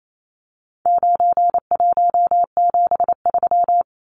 Morse code, heard as a keyed tone, about 28 wpm, 700 Hz. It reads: 9173
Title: Numbers - 4-Digit 28wpm